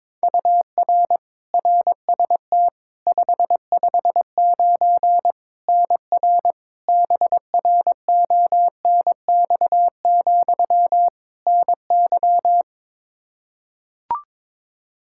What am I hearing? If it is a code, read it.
Morse code, heard as a keyed tone, about 23 wpm, 700 Hz. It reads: UR RST 559 NR BRONX, NY
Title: Practice QSO - Exchange 1 - Beginning - 22wpm